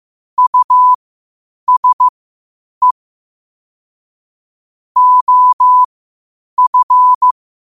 Morse code, heard as a keyed tone, about 15 wpm, spaced as for 8 wpm, 1000 Hz. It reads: USE OF
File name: 8wpm